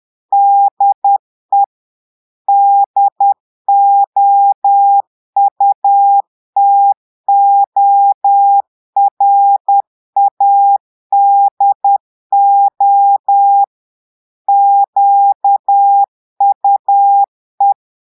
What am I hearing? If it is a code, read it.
Morse code, heard as a keyed tone, about 10 wpm, 800 Hz. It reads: DE DOUTORADO QUE
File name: Portugais - texte en clair 1 - vitesse - 10 mots minute